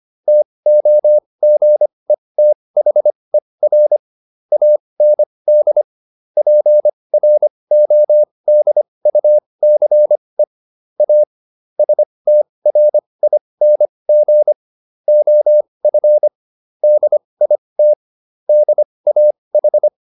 Morse code, heard as a keyed tone, about 25 wpm, spaced as for 20 wpm, 600 Hz. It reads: TOGETHER AND PRODUCE A STRING OF DIT DAH
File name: SS215 20